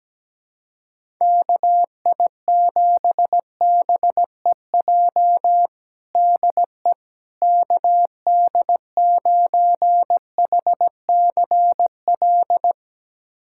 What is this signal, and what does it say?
Morse code, heard as a keyed tone, about 17 wpm, 700 Hz. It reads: KI7BEJ DE KD9HCL
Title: USA Callsign DE Callsign - 6 Characters - 17wpm